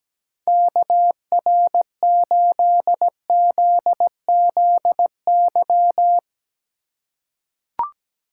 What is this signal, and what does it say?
Morse code, heard as a keyed tone, about 17 wpm, 700 Hz. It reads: KR8ZZY
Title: USA Call Signs - 3x Character Spacing - 17wpm